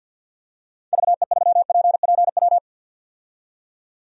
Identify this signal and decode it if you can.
Morse code, heard as a keyed tone, about 48 wpm, 700 Hz. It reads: VE3PPW
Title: DXCC-100 Call Signs - 6 Characters - 50wpm